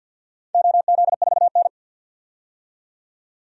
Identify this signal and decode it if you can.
Morse code, heard as a keyed tone, about 52 wpm, 700 Hz. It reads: OZ4N
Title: DXCC-100 Call Signs - 4 Characters - 50wpm